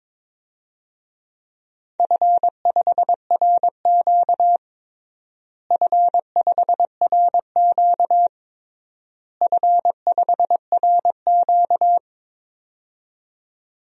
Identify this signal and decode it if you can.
Morse code, heard as a keyed tone, about 22 wpm, 700 Hz. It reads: F5RQ F5RQ F5RQ
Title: DXCC-100 Call Signs - 4 Characters - Repeated 3x - 22wpm